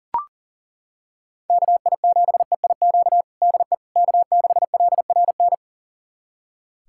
Morse code, heard as a keyed tone, about 40 wpm, 700 Hz. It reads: KI7EIQ DE K6LRN
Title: USA Callsign DE Callsign 40wpm